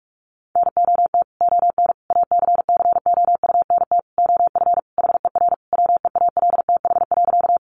Morse code, heard as a keyed tone, about 44 wpm, 700 Hz. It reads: NOT ON ACCOUNT OF HER WEALTH.
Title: Sentences from Top 1200 Words 45wpm